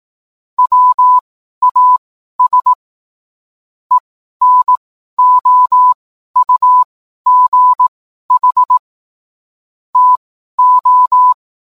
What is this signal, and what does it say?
Morse code, heard as a keyed tone, about 18 wpm, spaced as for 12 wpm, 1000 Hz. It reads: WAS ENOUGH TO